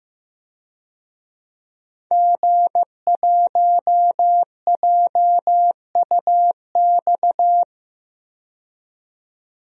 Morse code, heard as a keyed tone, about 15 wpm, 700 Hz. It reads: G1JUX
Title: DXCC-100 Call Signs - 5 Characters - 15wpm